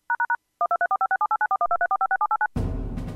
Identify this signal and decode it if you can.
DTMF, heard as telephone touch tones, tones of about 80 ms. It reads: #1234567894123456789